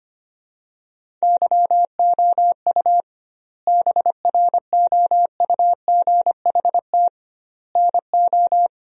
Morse code, heard as a keyed tone, about 25 wpm, 700 Hz. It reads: YOU BROUGHT NO
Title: 2x Word-Spacing - Sentences from Top 500 Words - 25wpm